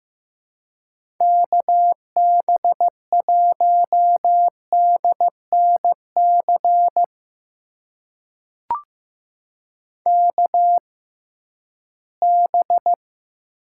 Morse code, heard as a keyed tone, about 15 wpm, spaced as for 11 wpm, 700 Hz. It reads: KB1DNC KB
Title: USA Call Signs - 6 Characters - 6x Character Spacing - 15wpm